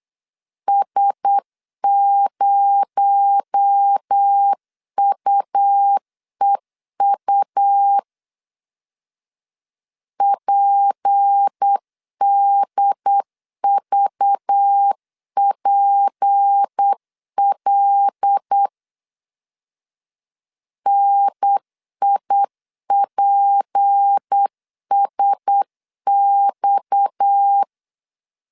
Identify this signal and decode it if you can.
Morse code, heard as a keyed tone, about 8 wpm, 800 Hz. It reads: S0UEU PDVPL NIPSX
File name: CW / DET / 1 6 1500C S C 8wpm 5219